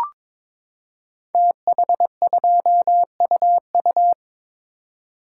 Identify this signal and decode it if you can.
Morse code, heard as a keyed tone, about 22 wpm, 690 Hz. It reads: TH2UU
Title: Code Groups - 5 Letter:Numbers - 22wpm